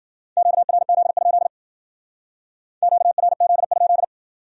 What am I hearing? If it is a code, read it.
Morse code, heard as a keyed tone, about 54 wpm, 700 Hz. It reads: QRZ? QRZ?